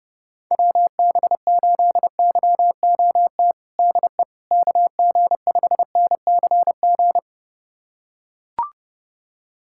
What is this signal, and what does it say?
Morse code, heard as a keyed tone, about 30 wpm, 700 Hz. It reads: WB8YOT DE KG5NCG
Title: USA Callsign DE Callsign 30wpm